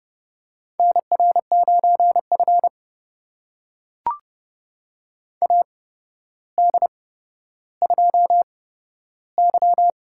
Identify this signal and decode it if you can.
Morse code, heard as a keyed tone, about 30 wpm, spaced as for 13 wpm, 700 Hz. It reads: NR9F AD2Y
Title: USA Call Signs - 4 Characters - 8x Character Spacing - 30wpm